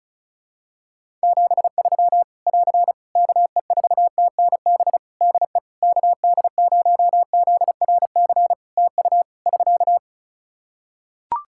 Morse code, heard as a keyed tone, about 36 wpm, 700 Hz. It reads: 73 <AR> KE4TNB DE KD0ZRC TU <SK>
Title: K1USN SST QSO Practice 35wpm